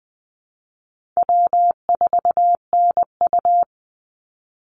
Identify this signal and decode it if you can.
Morse code, heard as a keyed tone, about 20 wpm, 700 Hz. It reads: W4NU
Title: Easy Call Signs Repeated 3x - 20wpm